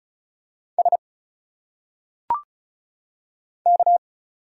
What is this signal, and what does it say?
Morse code, heard as a keyed tone, about 34 wpm, 700 Hz. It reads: S K
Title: Single Letters - Phonetic Pronunciation 35wpm